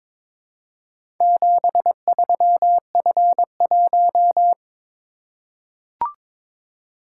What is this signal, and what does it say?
Morse code, heard as a keyed tone, about 22 wpm, 700 Hz. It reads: 73F1
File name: practice / 22wpm / Four Letter-Number Combinations - 22wpm